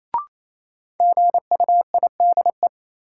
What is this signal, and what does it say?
Morse code, heard as a keyed tone, about 28 wpm, 700 Hz. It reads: GUIDE
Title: Sentences from Top 600 Words - New Word Drill - 28wpm